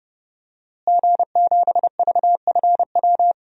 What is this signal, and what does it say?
Morse code, heard as a keyed tone, about 30 wpm, 700 Hz. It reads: G7VFW
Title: DXCC-100 Call Signs - 5 Characters - Repeated 3x - 30wpm